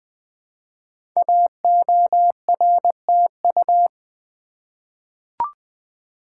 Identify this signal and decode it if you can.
Morse code, heard as a keyed tone, about 20 wpm, 700 Hz. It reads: AORTU
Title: Code Groups - 5 Letters - 20wpm